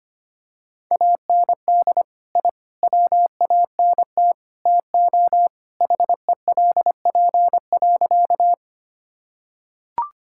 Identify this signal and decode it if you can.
Morse code, heard as a keyed tone, about 25 wpm, 700 Hz. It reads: AND I WANT TO HELP.
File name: practice / 25wpm / Sentences with 4-Word Phrases - 25wpm